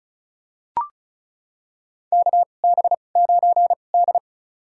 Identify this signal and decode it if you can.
Morse code, heard as a keyed tone, about 36 wpm, spaced as for 25 wpm, 700 Hz. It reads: KB9D
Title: USA Call Signs - 4 Characters - 2x Character Spacing - 35wpm